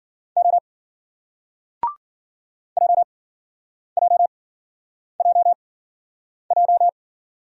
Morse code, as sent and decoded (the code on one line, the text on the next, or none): --- / .--- .--- .--- .---
O JJJJ